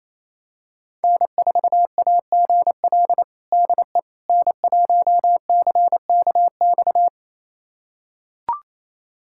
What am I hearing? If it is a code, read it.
Morse code, heard as a keyed tone, about 28 wpm, 700 Hz. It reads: N4AGL DE N1CKX
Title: USA Callsign DE Callsign - 5 Characters - 28wpm